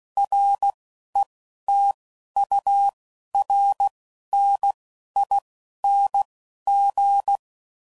Morse code, heard as a keyed tone, about 16 wpm, spaced as for 11 wpm, 800 Hz. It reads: RETURNING
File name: CW Code Groups @10WPM / Technical Subjects / XC